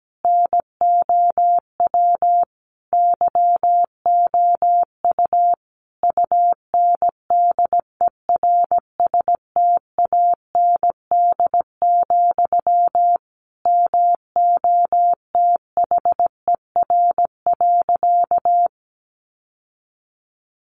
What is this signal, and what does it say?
Morse code, heard as a keyed tone, about 17 wpm, 700 Hz. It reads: NOW YOU UNDERSTAND, MOTHER.
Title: Sentences from Top 600 Words 17wpm